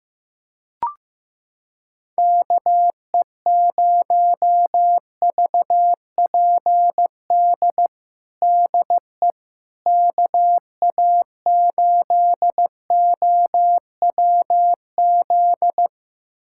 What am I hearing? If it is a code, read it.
Morse code, heard as a keyed tone, about 16 wpm, 700 Hz. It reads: KE0VPD DE KA8OWZ
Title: USA Callsign DE Callsign - 6 Characters - 15wpm